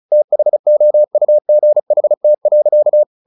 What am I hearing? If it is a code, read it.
Morse code, heard as a keyed tone, about 36 wpm, 600 Hz. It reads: THOUGHT.
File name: the old man and the sea 2 35wpm